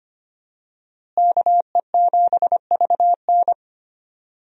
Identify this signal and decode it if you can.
Morse code, heard as a keyed tone, about 25 wpm, 700 Hz. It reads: KE7VN
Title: USA Call Signs - 5 Characters - Repeated 2x - 25wpm